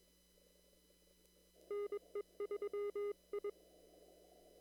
Morse code, heard as a keyed tone, about 21 wpm, 410 Hz. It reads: NE3I
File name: CW / kt5j wpx cw-op-k5pi